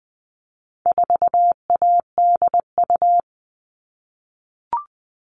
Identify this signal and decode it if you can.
Morse code, heard as a keyed tone, about 20 wpm, 700 Hz. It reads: 4ADU
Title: All Cut Numbers - 4-Digit 20wpm